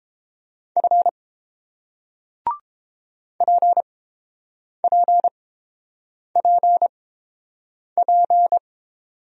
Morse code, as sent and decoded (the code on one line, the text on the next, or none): ..-. / .--. .--. .--. .--.
F PPPP